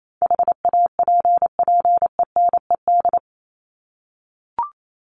Morse code, heard as a keyed tone, about 28 wpm, 700 Hz. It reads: HAPPENED
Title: ED Suffix - Top 100 Words with Base Words 28wpm